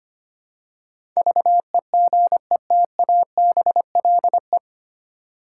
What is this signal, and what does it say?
Morse code, heard as a keyed tone, about 25 wpm, 700 Hz. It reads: VEGETABLE
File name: practice / 25wpm / 9 Letter Words - First 3 Letters Given - 25wpm